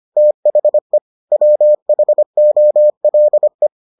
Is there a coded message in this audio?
Yes: Morse code from 0.2 to 3.7 s, heard as a keyed tone, about 25 wpm, 600 Hz. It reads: THE WHOLE